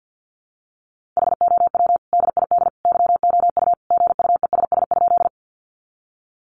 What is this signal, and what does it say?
Morse code, heard as a keyed tone, about 50 wpm, 700 Hz. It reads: HOW DID YOU GUESS?
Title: Sentences from Top 1500 Words 50wpm